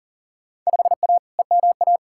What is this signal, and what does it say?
Morse code, heard as a keyed tone, about 40 wpm, 700 Hz. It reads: 5A EMA
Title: Field Day Exchanges - 40wpm